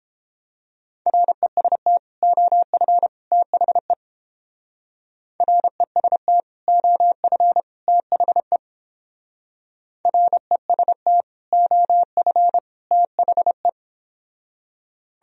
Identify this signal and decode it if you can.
Morse code, heard as a keyed tone, about 29 wpm, 700 Hz. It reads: REST OF THE REST OF THE REST OF THE